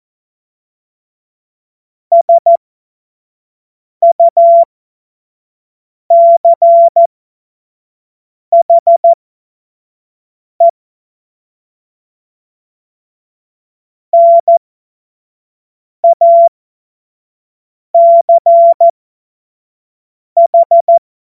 Morse code, heard as a keyed tone, about 14 wpm, spaced as for 5 wpm, 680 Hz. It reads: SUCHE NACH
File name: Morsekurs / Witze / Witz 07. März 2023 25 F70